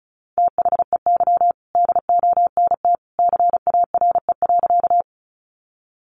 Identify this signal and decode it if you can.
Morse code, heard as a keyed tone, about 34 wpm, 700 Hz. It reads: THEY DONT CARE.
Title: Sentences from Top 300 Words 35wpm